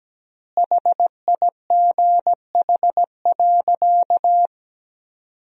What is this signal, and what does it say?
Morse code, heard as a keyed tone, about 17 wpm, 700 Hz. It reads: HIGH.
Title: 3x Word-Spacing - Sentences from Top 200 Words - 17wpm